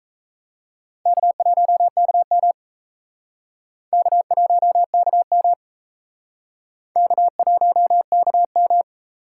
Morse code, heard as a keyed tone, about 37 wpm, 700 Hz. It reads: K1KM K1KM K1KM